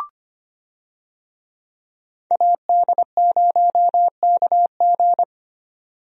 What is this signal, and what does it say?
Morse code, heard as a keyed tone, about 25 wpm, 700 Hz. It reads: AD0KG
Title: USA Call Signs - 5 Characters - Repeated 3x - 25wpm